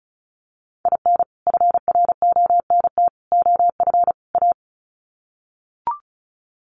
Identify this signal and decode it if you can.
Morse code, heard as a keyed tone, about 34 wpm, 700 Hz. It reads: IN FRONT OF A
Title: Sets of 4 Words 35wpm